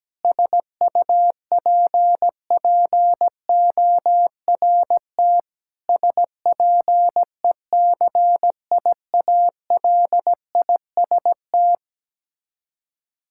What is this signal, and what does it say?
Morse code, heard as a keyed tone, about 17 wpm, 700 Hz. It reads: SUPPORT SPECIALIST
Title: Practice QSO - Exchange 3 - More Info - 17wpm